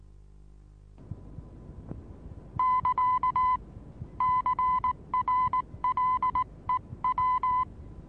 Morse code, heard as a keyed tone, about 19 wpm, 1010 Hz. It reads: <KA> CRLEW
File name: repos / Morsekurs / DJ4UF Morsen26